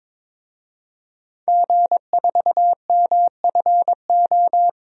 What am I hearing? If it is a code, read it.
Morse code, heard as a keyed tone, about 21 wpm, 700 Hz. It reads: G4MFO
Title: DXCC-100 Call Signs - 5 Characters - Repeated 2x - 22wpm